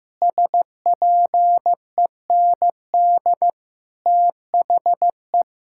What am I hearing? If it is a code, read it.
Morse code, heard as a keyed tone, about 14 wpm, 700 Hz. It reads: SPEND THE
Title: Sentences from Top 1000 Words - Review Part 5 - 15wpm